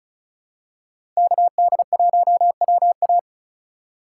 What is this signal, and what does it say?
Morse code, heard as a keyed tone, about 36 wpm, 700 Hz. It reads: KD1WA
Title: USA Call Signs - 5 Characters - 2x Character Spacing - 35wpm